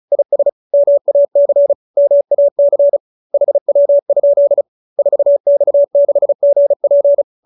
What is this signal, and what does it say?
Morse code, heard as a keyed tone, about 34 wpm, 570 Hz. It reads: IS MAC MAC HW? 4X6GP
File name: QSO310 35